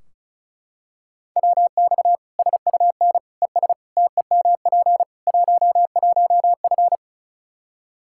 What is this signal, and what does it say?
Morse code, heard as a keyed tone, about 36 wpm, 700 Hz. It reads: WX SUN ES TEMP 11F
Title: Practice QSO - Exchange 2 - Info - 35wpm